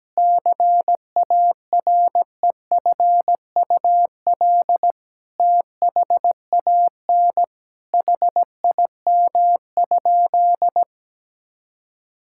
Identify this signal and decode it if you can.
Morse code, heard as a keyed tone, about 17 wpm, 700 Hz. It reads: CAREFUL THAN HIM?
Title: Sentences from Top 1600 Words 17wpm